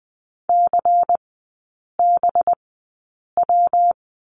Morse code, heard as a keyed tone, about 20 wpm, spaced as for 8 wpm, 700 Hz. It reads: CBW
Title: 184 Lesson 30e - Letter b - Three Characters 20wpm